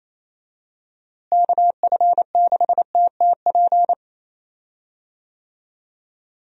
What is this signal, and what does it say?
Morse code, heard as a keyed tone, about 28 wpm, 700 Hz. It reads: KF6TTP
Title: USA Call Signs - 6 Characters - Repeated 3x - 28wpm